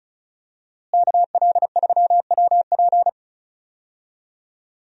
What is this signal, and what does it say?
Morse code, heard as a keyed tone, about 36 wpm, 700 Hz. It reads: KL3WP
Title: USA Call Signs - 5 Characters - Repeated 2x - 35wpm